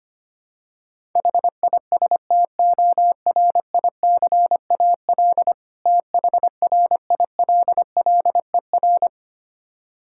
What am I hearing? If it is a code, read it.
Morse code, heard as a keyed tone, about 25 wpm, 700 Hz. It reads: HISTORICAL THRILLER
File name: practice / 25wpm / Difficult Words with Hints - 25wpm